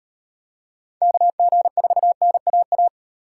Morse code, heard as a keyed tone, about 35 wpm, 700 Hz. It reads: KG4NAA